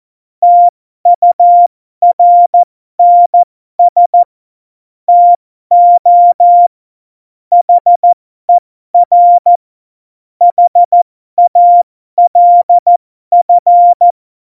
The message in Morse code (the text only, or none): TURNS TO HER HALF